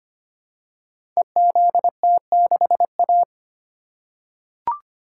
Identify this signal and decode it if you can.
Morse code, heard as a keyed tone, about 25 wpm, 700 Hz. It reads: EZT6A
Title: Code Groups - 5 Letter:Numbers - 25wpm